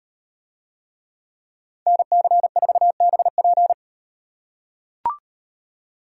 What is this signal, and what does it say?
Morse code, heard as a keyed tone, about 38 wpm, 700 Hz. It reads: NC4BP